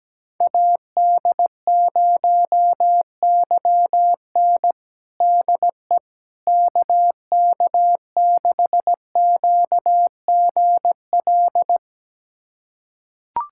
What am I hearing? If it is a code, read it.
Morse code, heard as a keyed tone, about 17 wpm, 700 Hz. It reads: AD0YN DE KK6QGL
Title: USA Callsign DE Callsign 17wpm